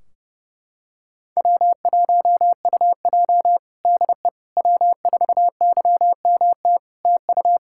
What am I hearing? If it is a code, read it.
Morse code, heard as a keyed tone, about 30 wpm, 700 Hz. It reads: W1UJ DE W4YMT TU
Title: Practice QSO - Exchange 4 - Ending - 30wpm